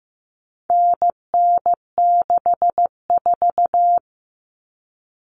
Morse code, heard as a keyed tone, about 15 wpm, 700 Hz. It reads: NN64
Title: Common Cut Numbers - 4-Digit 15wpm